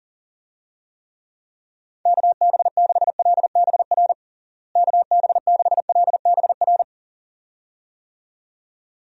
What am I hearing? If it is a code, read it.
Morse code, heard as a keyed tone, about 40 wpm, 700 Hz. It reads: KB6LBR KB6LBR
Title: USA Call Signs - 6 Characters - Repeated 2x - 40wpm